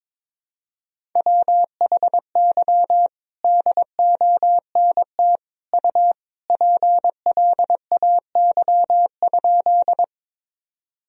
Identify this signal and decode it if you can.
Morse code, heard as a keyed tone, about 21 wpm, 700 Hz. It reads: WHY DONT U PLAY?
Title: Sentences from Top 300 Words and Abbreviations 22wpm